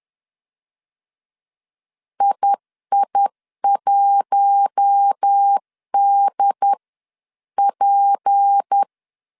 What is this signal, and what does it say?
Morse code, heard as a keyed tone, about 10 wpm, spaced as for 9 wpm, 800 Hz. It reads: II1DP
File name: CW / DET / 1 6 1500C S C 10wpm 4252